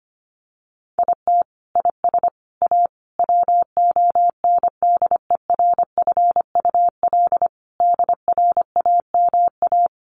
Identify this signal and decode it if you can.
Morse code, heard as a keyed tone, about 25 wpm, 700 Hz. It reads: IT IS A WONDERFUL DRAMA
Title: Sentences from Top 1500 Words 25wpm